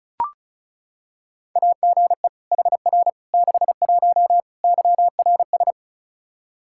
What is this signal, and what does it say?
Morse code, heard as a keyed tone, about 34 wpm, 700 Hz. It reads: AGE HR 61 YRS
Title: Practice QSO - Exchange 3 - More Info - 35wpm